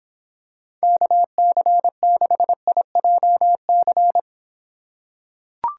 Morse code, heard as a keyed tone, about 24 wpm, 700 Hz. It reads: KC6IJC